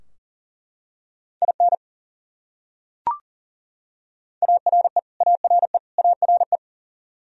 Morse code, heard as a keyed tone, about 40 wpm, 700 Hz. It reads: IN ARE ARE ARE
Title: Top 100 Words Repeated 40wpm